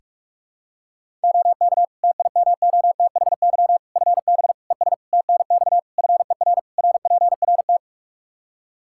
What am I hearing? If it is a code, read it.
Morse code, heard as a keyed tone, about 46 wpm, 700 Hz. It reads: OK TIMOTHY FB ES TNX FER RPRT
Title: Practice QSO - Exchange 2 - Info - 45wpm